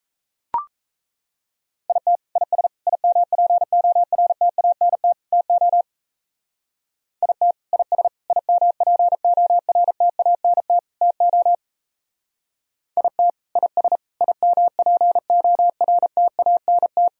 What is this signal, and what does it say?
Morse code, heard as a keyed tone, about 36 wpm, 700 Hz. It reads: IT IS IMPORTANT TO IT IS IMPORTANT TO IT IS IMPORTANT